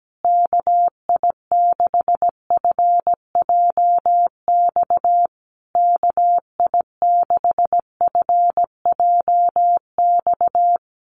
Morse code, heard as a keyed tone, about 17 wpm, 700 Hz. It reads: KI6FJX KI6FJX
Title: USA Call Signs Repeated 3x 17wpm